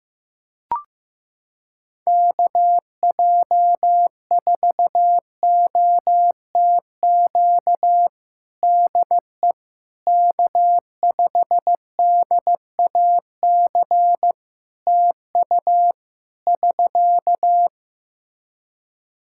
Morse code, heard as a keyed tone, about 15 wpm, 700 Hz. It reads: KJ4OTQ DE K5DAC TU <SK>
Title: Practice QSO - Exchange 4 - Ending - 15wpm